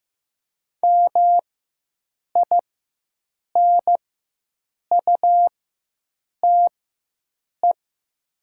Morse code, heard as a keyed tone, about 15 wpm, spaced as for 7 wpm, 700 Hz. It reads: MINUTE
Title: 4x Character Spacing - Top 500 Words - 15wpm